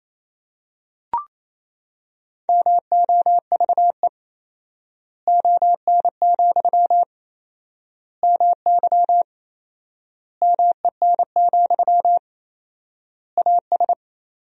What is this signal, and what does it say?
Morse code, heard as a keyed tone, about 28 wpm, 700 Hz. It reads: MOVE ON, MY MEN, AS
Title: 4x Word-Spacing - Sentences from Top 500 Words - 28wpm